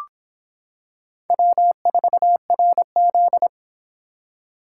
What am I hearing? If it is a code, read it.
Morse code, heard as a keyed tone, about 25 wpm, 700 Hz. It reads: W4RZ